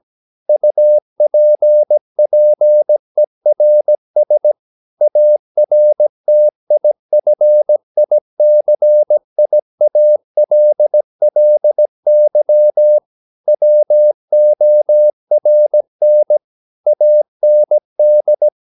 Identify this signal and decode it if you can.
Morse code, heard as a keyed tone, about 17 wpm, 600 Hz. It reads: UPPERS ARTIFICIALLY WORN AND
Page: Romanen Never af Ken Follet på engelsk.